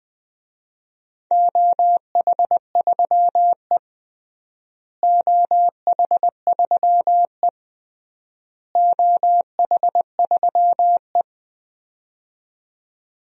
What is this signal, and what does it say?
Morse code, heard as a keyed tone, about 20 wpm, 700 Hz. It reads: OH3E OH3E OH3E
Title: DXCC-100 Call Signs - 4 Characters - Repeated 3x - 20wpm